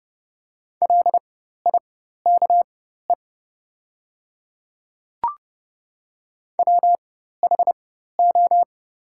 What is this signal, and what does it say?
Morse code, heard as a keyed tone, about 30 wpm, spaced as for 14 wpm, 700 Hz. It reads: LIKE WHO
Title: 4x Character Spacing - Top 100 Words - 30wpm